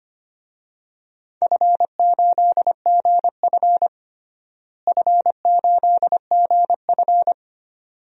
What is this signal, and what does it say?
Morse code, heard as a keyed tone, about 25 wpm, 700 Hz. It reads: F8GF F8GF
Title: DXCC-100 Call Signs - 4 Characters - Repeated 2x - 25wpm